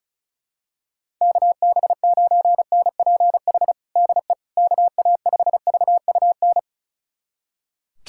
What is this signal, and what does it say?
Morse code, heard as a keyed tone, about 36 wpm, 700 Hz. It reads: KB9NPH DE KA5VUN
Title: USA Callsign DE Callsign 35wpm